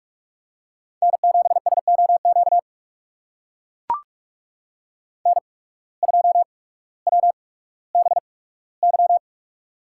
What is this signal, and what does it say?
Morse code, heard as a keyed tone, about 44 wpm, spaced as for 19 wpm, 700 Hz. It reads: N7SOQ N2WBY
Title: USA Call Signs - 5 Characters - 8x Character Spacing - 45wpm